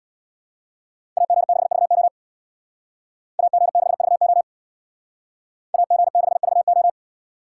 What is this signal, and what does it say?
Morse code, heard as a keyed tone, about 61 wpm, 700 Hz. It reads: AG6VO AG6VO AG6VO
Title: Speed Racing - USA 2x2 Call Signs 50wpm